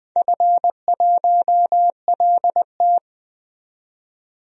Morse code, heard as a keyed tone, about 20 wpm, 700 Hz. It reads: F1LT
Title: DXCC-100 Call Signs - 4 Characters - 20wpm